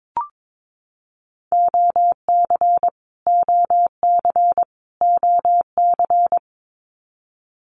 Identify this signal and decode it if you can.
Morse code, heard as a keyed tone, about 22 wpm, 700 Hz. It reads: OC OC OC
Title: Abbreviations Repeated 3x - 22wpm